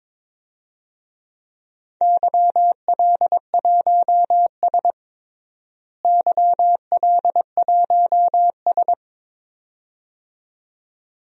Morse code, heard as a keyed tone, about 22 wpm, 700 Hz. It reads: YL1S YL1S
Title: DXCC-100 Call Signs - 4 Characters - Repeated 2x - 22wpm